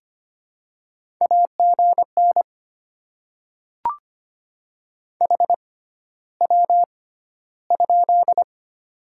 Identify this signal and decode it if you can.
Morse code, heard as a keyed tone, about 25 wpm, spaced as for 13 wpm, 700 Hz. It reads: AGN HW?
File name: Top 90 QSO Elements - 6x Character Spacing - 25wpm